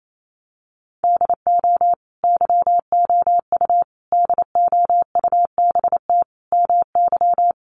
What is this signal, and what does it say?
Morse code, heard as a keyed tone, about 28 wpm, 700 Hz. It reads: DO YOU DOUBT MY
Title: Sentences from Top 1500 Words 28wpm